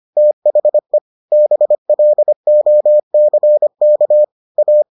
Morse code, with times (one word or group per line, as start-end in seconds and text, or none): 0.2-1.0 THE
1.3-4.2 BLOCK
4.6-4.8 A